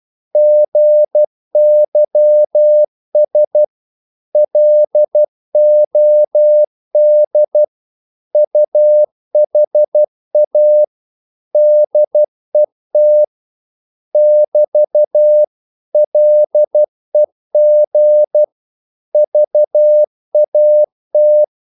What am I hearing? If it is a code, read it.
Morse code, heard as a keyed tone, about 12 wpm, 600 Hz. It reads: GYS LOD UHA DET <BT> LEG VAT
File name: Korte ord DK 12wpm